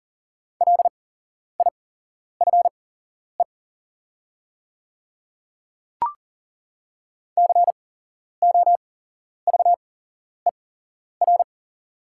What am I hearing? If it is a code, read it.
Morse code, heard as a keyed tone, about 40 wpm, spaced as for 11 wpm, 690 Hz. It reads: LIFE COVER
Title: 8x Character Spacing - Top 300 Words - 40wpm